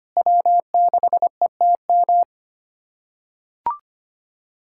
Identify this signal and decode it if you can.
Morse code, heard as a keyed tone, about 25 wpm, 700 Hz. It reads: W6ETM
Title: USA Callsign DE Callsign - 5 Characters - 25wpm